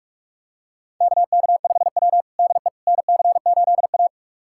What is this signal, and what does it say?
Morse code, heard as a keyed tone, about 43 wpm, 700 Hz. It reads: KK5W DE NC8A
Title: USA Callsign DE Callsign - 4 Characters - 45wpm